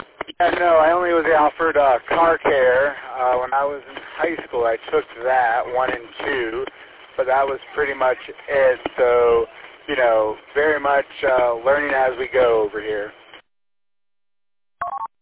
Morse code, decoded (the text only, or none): none